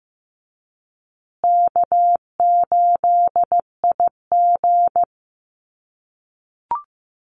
Morse code, heard as a keyed tone, about 15 wpm, 700 Hz. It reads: K8IG
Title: USA Call Signs Repeated 3x 15wpm